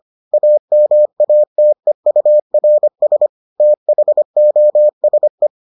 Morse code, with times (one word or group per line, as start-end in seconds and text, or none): 0.3-3.3 AMATEURS
3.6-5.5 THOSE